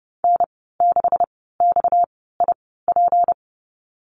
Morse code, heard as a keyed tone, about 30 wpm, spaced as for 17 wpm, 700 Hz. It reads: N6XIP
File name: Lesson 37g - Letter x - Call Signs 30wpm